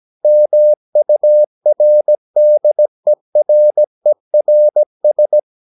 Morse code, heard as a keyed tone, about 17 wpm, 600 Hz. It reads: MURDERERS
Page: Romanen Never af Ken Follet på engelsk.